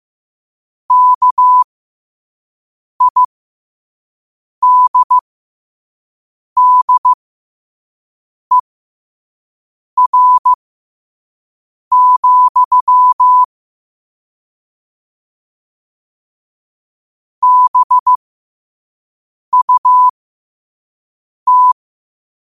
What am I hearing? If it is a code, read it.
Morse code, heard as a keyed tone, about 15 wpm, spaced as for 5 wpm, 1000 Hz. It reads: KIDDER, BUT
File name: qotdfiles / 5wpm